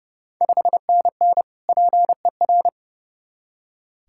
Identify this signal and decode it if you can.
Morse code, heard as a keyed tone, about 30 wpm, 700 Hz. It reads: 5NN PER
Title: Ontario QSO Party Exchange 30wpm